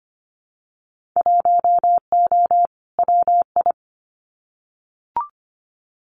Morse code, heard as a keyed tone, about 25 wpm, 700 Hz. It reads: 1O WI
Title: Winter Field Day Exchanges 25wpm